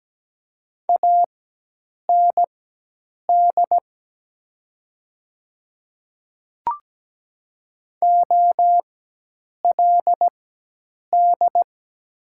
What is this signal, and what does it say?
Morse code, heard as a keyed tone, about 17 wpm, spaced as for 8 wpm, 700 Hz. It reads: AND OLD
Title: 4x Character Spacing - Top 200 Words - 17wpm